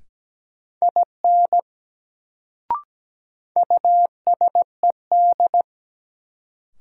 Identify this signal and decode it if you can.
Morse code, heard as a keyed tone, about 17 wpm, 700 Hz. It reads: IN USED
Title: Sets of 2 Words - New Word Drill - 17wpm